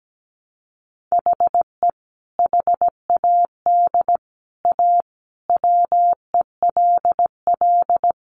Morse code, heard as a keyed tone, about 17 wpm, 700 Hz. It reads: HE HAD A WELL
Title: Sentences from Top 1300 Words 17wpm